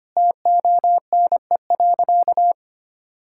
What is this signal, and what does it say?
Morse code, heard as a keyed tone, about 25 wpm, 700 Hz. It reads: TONE.
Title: Sentences from Top 1600 Words 25wpm